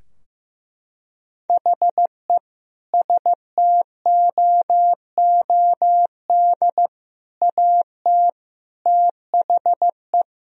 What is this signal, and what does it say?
Morse code, heard as a keyed tone, about 15 wpm, 700 Hz. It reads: HE STOOD AT THE
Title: Sentences from Top 1000 Words - Review Part 3 - 15wpm